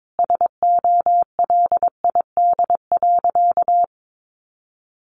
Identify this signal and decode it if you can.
Morse code, heard as a keyed tone, about 22 wpm, 700 Hz. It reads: SOLID.
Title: Sentences from Top 1500 Words 22wpm